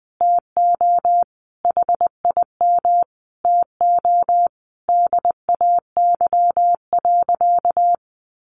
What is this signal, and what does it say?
Morse code, heard as a keyed tone, about 19 wpm, 700 Hz. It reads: TO HIM TO DAY.
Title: Sentences from Top 1100 Words 20wpm